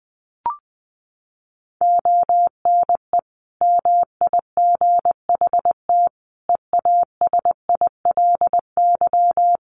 Morse code, heard as a keyed tone, about 20 wpm, 700 Hz. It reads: ONE MIGHT EASILY
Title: Sentences from Top 1400 Words 20wpm